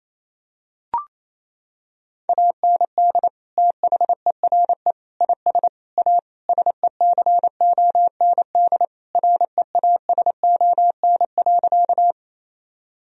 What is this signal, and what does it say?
Morse code, heard as a keyed tone, about 28 wpm, 700 Hz. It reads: AND THERE IS A SECOND REASON.
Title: Sentences from Top 1000 Words - Review Part 1 - 28wpm